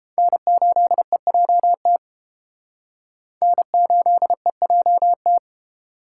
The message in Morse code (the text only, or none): N8EJT N8EJT